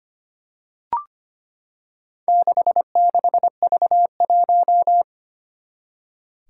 Morse code, heard as a keyed tone, about 25 wpm, 700 Hz. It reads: 66V1
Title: Four Challenging Letter-Number Combinations 25wpm